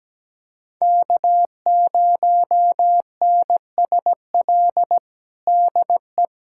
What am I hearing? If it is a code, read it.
Morse code, heard as a keyed tone, about 17 wpm, 700 Hz. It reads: K0NSL DE
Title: USA Callsign DE Callsign - 5 Characters - 17wpm